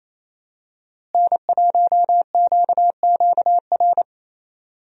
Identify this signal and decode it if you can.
Morse code, heard as a keyed tone, about 28 wpm, 700 Hz. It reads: N1QQR
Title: USA Call Signs - 5 Characters - Repeated 2x - 28wpm